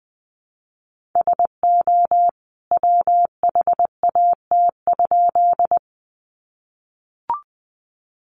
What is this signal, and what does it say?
Morse code, heard as a keyed tone, about 20 wpm, 700 Hz. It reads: SO WHAT?
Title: A Sound Of Thunder 20wpm